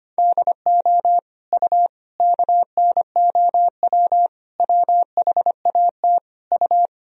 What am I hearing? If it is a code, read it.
Morse code, heard as a keyed tone, about 25 wpm, 700 Hz. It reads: DO U KNOW WHAT U
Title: Sentences from Top 100 Words and Abbreviations 25wpm